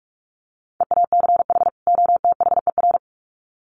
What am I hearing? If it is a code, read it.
Morse code, heard as a keyed tone, about 44 wpm, 700 Hz. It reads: EACH OTHER
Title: Sets of 2 Words 45wpm